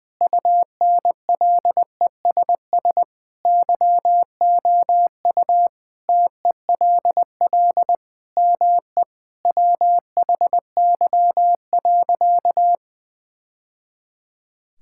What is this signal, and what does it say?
Morse code, heard as a keyed tone, about 20 wpm, 700 Hz. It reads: UNLESS YOU TELL ME WHY.
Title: Sentences from Top 1000 Words - Review Part 6 - 20wpm